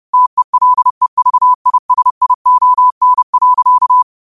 Morse code, heard as a keyed tone, about 30 wpm, 1000 Hz. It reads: TELEVISION.
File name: qotdfiles / 30wpm